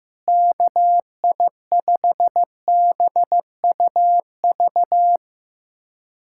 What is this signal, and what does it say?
Morse code, heard as a keyed tone, about 15 wpm, 700 Hz. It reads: KI5BUV
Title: USA Callsign DE Callsign - 6 Characters - 15wpm